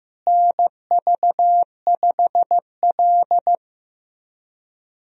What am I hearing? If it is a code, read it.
Morse code, heard as a keyed tone, about 15 wpm, 700 Hz. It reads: NV5L
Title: USA Call Signs - 4 Characters - Repeated 3x - 15wpm